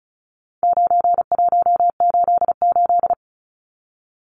Morse code, heard as a keyed tone, about 34 wpm, 700 Hz. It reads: 9188
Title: Numbers - 4-Digit 35wpm